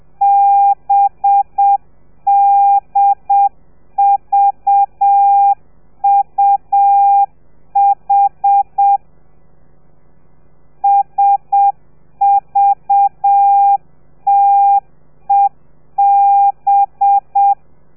Morse code, heard as a keyed tone, about 7 wpm, 800 Hz. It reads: BDVUH SVTEB
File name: telegrafia / i2tfj / Lez4 30